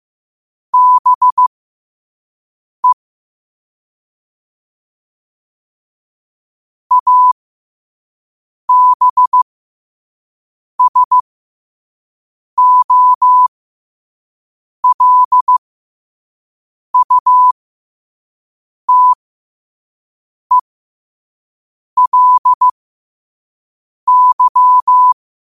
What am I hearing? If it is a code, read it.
Morse code, heard as a keyed tone, about 15 wpm, spaced as for 5 wpm, 1000 Hz. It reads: BE ABSOLUTELY
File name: qotdfiles / 5wpm